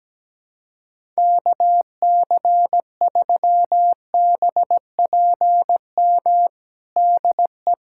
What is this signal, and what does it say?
Morse code, heard as a keyed tone, about 17 wpm, 700 Hz. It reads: KC3BPM DE
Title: USA Callsign DE Callsign - 6 Characters - 17wpm